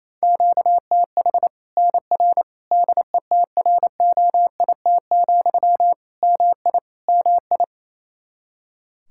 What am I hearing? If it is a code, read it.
Morse code, heard as a keyed tone, about 27 wpm, 700 Hz. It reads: QTH NR DETROIT, MI MI
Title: Practice QSO - Exchange 1 - Beginning - 28wpm